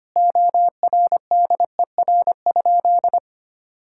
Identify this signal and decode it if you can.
Morse code, heard as a keyed tone, about 25 wpm, 700 Hz. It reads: ORDER?
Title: Sentences from Top 1600 Words 25wpm